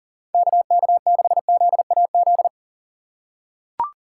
Morse code, heard as a keyed tone, about 40 wpm, 700 Hz. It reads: KK6ZAZ
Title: USA Call Signs - 4x Character Spacing - 40wpm